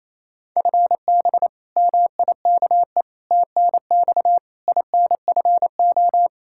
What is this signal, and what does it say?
Morse code, heard as a keyed tone, about 28 wpm, 700 Hz. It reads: FB MIKE TNX INFO
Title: Practice QSO - Exchange 3 - More Info - 28wpm